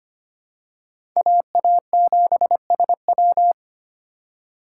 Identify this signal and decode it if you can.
Morse code, heard as a keyed tone, about 25 wpm, 700 Hz. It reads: AA7SW
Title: USA Call Signs - 5 Characters - 4x Character Spacing - 25wpm